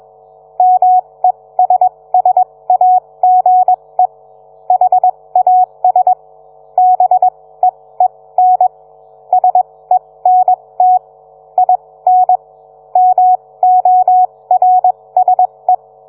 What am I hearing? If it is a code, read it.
Morse code, heard as a keyed tone, about 21 wpm, spaced as for 17 wpm, 740 Hz. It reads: MESSAGE HAS BEEN SENT IN MORSE